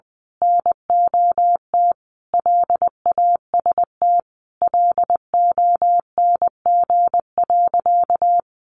Morse code, heard as a keyed tone, about 20 wpm, 700 Hz. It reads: NOT LAST LONG.
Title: Sentences from Top 1300 Words 20wpm